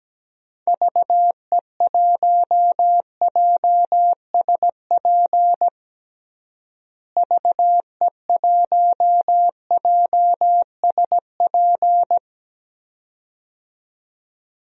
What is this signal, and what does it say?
Morse code, heard as a keyed tone, about 17 wpm, 700 Hz. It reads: VE1JSP VE1JSP
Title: DXCC-100 Call Signs - 6 Characters - Repeated 2x - 17wpm